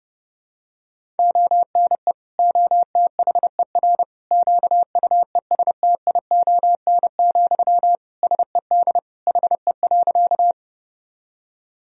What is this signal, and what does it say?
Morse code, heard as a keyed tone, about 31 wpm, 700 Hz. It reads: ONE OTHER QUESTION, SED HE.
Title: Sentences from Top 400 Words and Abbreviations 30wpm